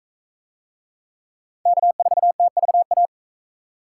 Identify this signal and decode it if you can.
Morse code, heard as a keyed tone, about 39 wpm, 700 Hz. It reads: K4TVA